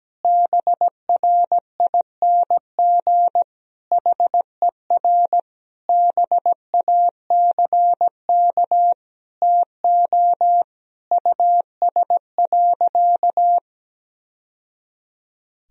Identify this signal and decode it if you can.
Morse code, heard as a keyed tone, about 17 wpm, 700 Hz. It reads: BRING HER BACK TO US.
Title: Sentences from Top 1000 Words - Review Part 3 - 17wpm